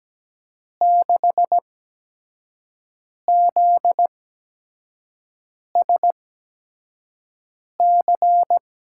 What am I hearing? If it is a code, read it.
Morse code, heard as a keyed tone, about 17 wpm, spaced as for 5 wpm, 700 Hz. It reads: 6ZSC
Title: USA Call Signs - 6 Characters - 8x Character Spacing - 17wpm